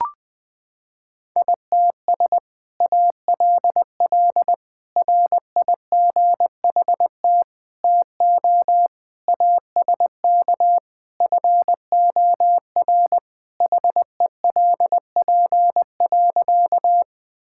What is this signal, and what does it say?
Morse code, heard as a keyed tone, about 21 wpm, 700 Hz. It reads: ITS ALL RIGHT TO ASK FOR HELP.
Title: Sentences with 2-Word Phrases - 20wpm